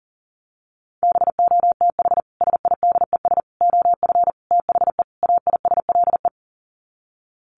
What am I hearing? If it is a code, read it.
Morse code, heard as a keyed tone, about 40 wpm, 700 Hz. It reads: BOTH SIDES OF THE AISLE
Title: Sets of 5 Words Encore2 40wpm